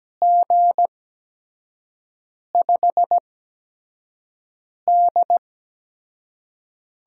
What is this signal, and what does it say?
Morse code, heard as a keyed tone, about 17 wpm, spaced as for 5 wpm, 700 Hz. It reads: G5D
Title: USA Call Signs - 6 Characters - 8x Character Spacing - 17wpm